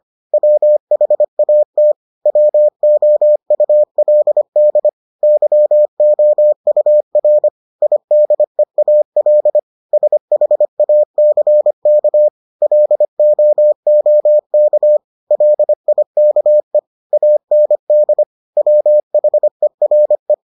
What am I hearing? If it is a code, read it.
Morse code, heard as a keyed tone, about 25 wpm, 600 Hz. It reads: WHAT WOULD YOUR IDEAL SHACK LOOK LIKE AND WHERE